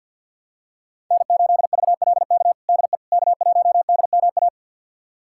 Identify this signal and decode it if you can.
Morse code, heard as a keyed tone, about 46 wpm, 700 Hz. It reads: N8VLK DE K1DMA
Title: USA Callsign DE Callsign - 5 Characters - 50wpm